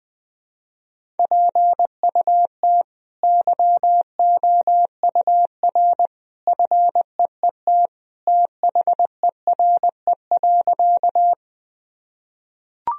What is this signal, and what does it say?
Morse code, heard as a keyed tone, about 20 wpm, 700 Hz. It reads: PUT YOUR FEET THERE.
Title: Sentences from Top 1000 Words - Review Part 1 - 20wpm